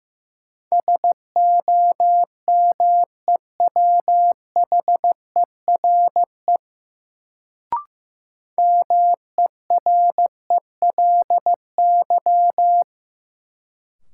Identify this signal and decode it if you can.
Morse code, heard as a keyed tone, about 16 wpm, 700 Hz. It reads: SOMEWHERE MERELY
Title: Sentences from Top 1400 Words - New Word Drill - 15wpm